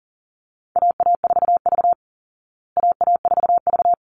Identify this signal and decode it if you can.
Morse code, heard as a keyed tone, about 40 wpm, 700 Hz. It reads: AA4V AA4V
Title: Call Signs Repeated 3x - 40wpm